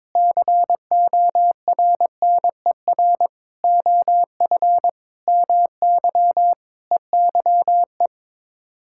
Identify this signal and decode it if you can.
Morse code, heard as a keyed tone, about 21 wpm, 700 Hz. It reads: CORNER OF MY EYE
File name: practice / 22wpm / Sets of 7 Words - 22wpm